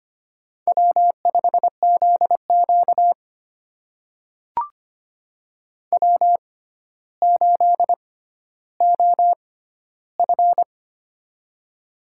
Morse code, heard as a keyed tone, about 25 wpm, spaced as for 13 wpm, 700 Hz. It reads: W5ZQ W8OF
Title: USA Call Signs - 4 Characters - 6x Character Spacing - 25wpm